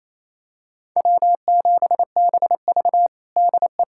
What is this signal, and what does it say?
Morse code, heard as a keyed tone, about 28 wpm, 700 Hz. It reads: W7BV DE
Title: USA Callsign DE Callsign - 4 Characters - 28wpm